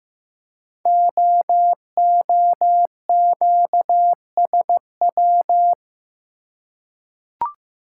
Morse code, heard as a keyed tone, about 15 wpm, 700 Hz. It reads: OOQSW
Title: Code Groups - 5 Letters - 15wpm